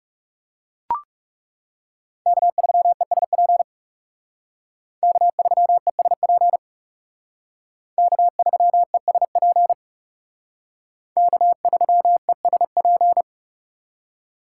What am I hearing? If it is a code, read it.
Morse code, heard as a keyed tone, about 36 wpm, 700 Hz. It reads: K3ESP K3ESP K3ESP K3ESP
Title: Speed Racing - USA 1x3 Call Signs 30wpm